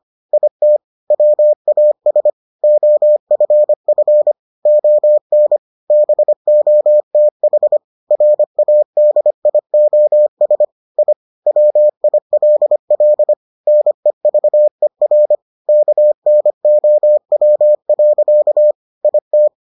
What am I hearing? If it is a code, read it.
Morse code, heard as a keyed tone, about 25 wpm, 600 Hz. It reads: IT WAS OFF ON BOTH RADIOS I WILL NEVER KNOW. IT